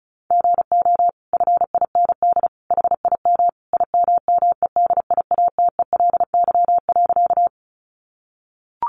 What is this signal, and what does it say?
Morse code, heard as a keyed tone, about 34 wpm, 700 Hz. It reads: GO FIND HIM IMMEDIATELY.
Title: Sentences from Top 1100 Words 35wpm